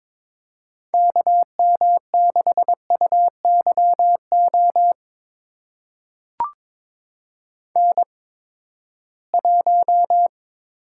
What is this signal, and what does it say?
Morse code, heard as a keyed tone, about 22 wpm, spaced as for 15 wpm, 700 Hz. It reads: KM6UYO N1
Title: USA Call Signs - 8x Character Spacing - 22wpm